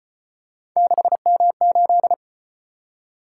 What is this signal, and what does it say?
Morse code, heard as a keyed tone, about 35 wpm, 700 Hz. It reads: 6M8